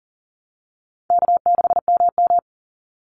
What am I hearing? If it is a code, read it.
Morse code, heard as a keyed tone, about 40 wpm, 700 Hz. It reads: K6MM
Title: Call Signs Repeated 2x - 40wpm